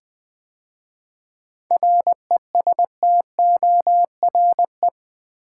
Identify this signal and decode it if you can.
Morse code, heard as a keyed tone, about 20 wpm, 700 Hz. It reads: RESTORE
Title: 7 Letter Words - First 3 Letters Given - 20wpm